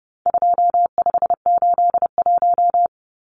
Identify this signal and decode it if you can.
Morse code, heard as a keyed tone, about 30 wpm, 700 Hz. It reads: 2581
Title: Numbers - 4-Digit 30wpm